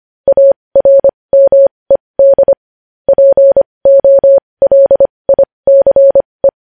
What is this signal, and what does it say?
Morse code, heard as a keyed tone, about 25 wpm, spaced as for 20 wpm, 550 Hz. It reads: ARMED POLICE
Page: news-headlines-in-morse-code-at-20-wpm